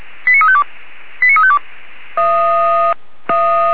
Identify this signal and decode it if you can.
Five-tone selective call (ZVEI), heard as quick runs of five tones, tones of about 70 ms. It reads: 89352 89352
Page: ZVEI